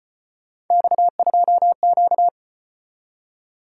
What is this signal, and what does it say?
Morse code, heard as a keyed tone, about 33 wpm, 700 Hz. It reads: X2Q